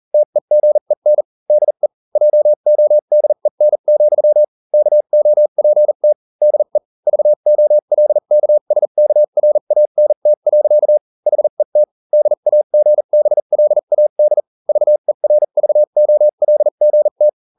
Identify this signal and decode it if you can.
Morse code, heard as a keyed tone, about 40 wpm, spaced as for 35 wpm, 600 Hz. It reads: TEGEN DE JODEN, KOPT DE VOLKSKRANT. HET DAGBLAD VERVOLGT